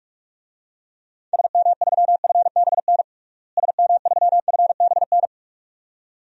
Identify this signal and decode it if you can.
Morse code, heard as a keyed tone, about 47 wpm, 700 Hz. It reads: SM3FBN SM3FBN
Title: DXCC-100 Call Signs - 6 Characters - Repeated 2x - 45wpm